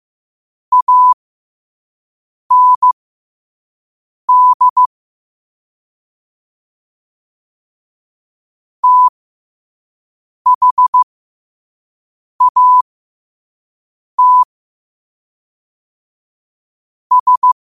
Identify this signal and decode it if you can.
Morse code, heard as a keyed tone, about 15 wpm, spaced as for 5 wpm, 1000 Hz. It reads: AND THAT S
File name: 2026-02-15-5wpm